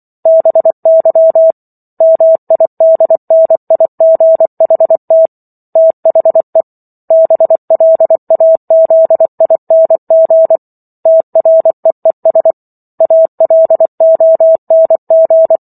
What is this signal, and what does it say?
Morse code, heard as a keyed tone, about 24 wpm, 650 Hz. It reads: BY MIDNIGHT THE BLAZING TREES ALONG